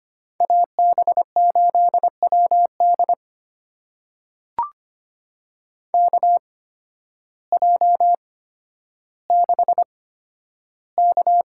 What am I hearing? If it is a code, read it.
Morse code, heard as a keyed tone, about 25 wpm, spaced as for 12 wpm, 700 Hz. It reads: AB8WD KJ6K
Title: USA Call Signs - 8x Character Spacing - 25wpm